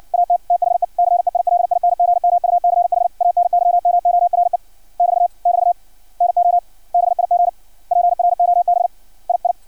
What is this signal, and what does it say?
Morse code, heard as a keyed tone, about 56 wpm, 700 Hz. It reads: AT THE CEILING MUCH ANYMORE <BT> <BT> NO BIG PROB II
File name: archived / CRSnet